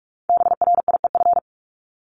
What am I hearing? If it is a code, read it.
Morse code, heard as a keyed tone, about 46 wpm, 700 Hz. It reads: BRIEF
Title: Multi-letter Phonemes 45wpm